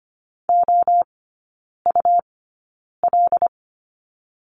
Morse code, heard as a keyed tone, about 25 wpm, 700 Hz. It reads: O U L
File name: Lesson 17e - Letter u - Three Characters 25wpm